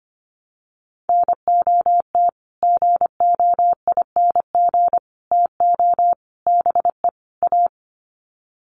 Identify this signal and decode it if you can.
Morse code, heard as a keyed tone, about 25 wpm, 700 Hz. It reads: NOT GOING TO BE A
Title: Sets of 5 Words Encore 25wpm